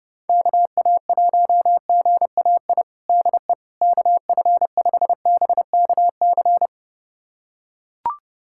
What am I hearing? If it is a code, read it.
Morse code, heard as a keyed tone, about 30 wpm, 700 Hz. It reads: KA1GAI DE KF5BKC
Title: USA Callsign DE Callsign - 6 Characters - 30wpm